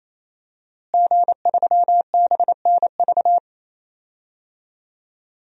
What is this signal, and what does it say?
Morse code, heard as a keyed tone, about 28 wpm, 700 Hz. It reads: G3BNV
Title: DXCC-100 Call Signs - 5 Characters - Repeated 3x - 28wpm